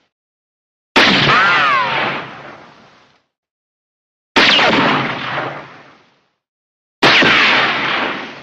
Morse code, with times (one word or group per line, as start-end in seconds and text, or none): none